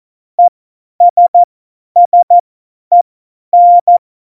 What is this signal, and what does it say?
Morse code, heard as a keyed tone, about 14 wpm, spaced as for 10 wpm, 700 Hz. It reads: ESSEN
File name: Morsekurs / Witze / Witz 14. Januar 2020, 50 F70